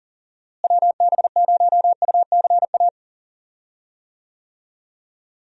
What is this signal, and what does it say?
Morse code, heard as a keyed tone, about 40 wpm, 700 Hz. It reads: WB0UCA
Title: USA Call Signs - 6 Characters - Repeated 2x - 40wpm